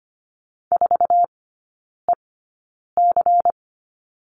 Morse code, heard as a keyed tone, about 25 wpm, spaced as for 9 wpm, 700 Hz. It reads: 4EC
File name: course / 25:10wpm / 093 Lesson 16e - Letter c - Three Characters 25wpm